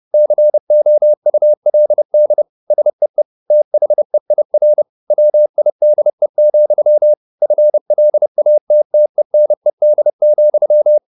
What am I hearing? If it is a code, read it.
Morse code, heard as a keyed tone, about 30 wpm, 600 Hz. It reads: COULD SEE THEIR WIDE, FLATTENED,